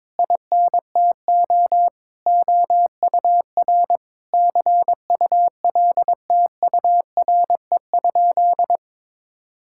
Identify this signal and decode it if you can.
Morse code, heard as a keyed tone, about 22 wpm, 700 Hz. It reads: INTO OUR CULTURE?
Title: Sentences from Top 1000 Words - Review Part 6 - 22wpm